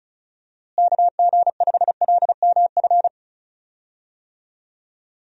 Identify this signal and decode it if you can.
Morse code, heard as a keyed tone, about 34 wpm, 700 Hz. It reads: KG5LMF
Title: USA Call Signs - 6 Characters - Repeated 2x - 35wpm